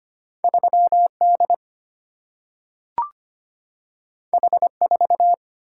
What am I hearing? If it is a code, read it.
Morse code, heard as a keyed tone, about 25 wpm, 700 Hz. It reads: 3 D H 4
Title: Single-Letter Single-Number 25wpm